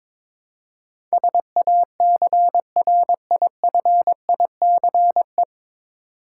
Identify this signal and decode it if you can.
Morse code, heard as a keyed tone, about 22 wpm, 700 Hz. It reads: SACRIFICE
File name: 9 Letter Words - First 3 Letters Given - 22wpm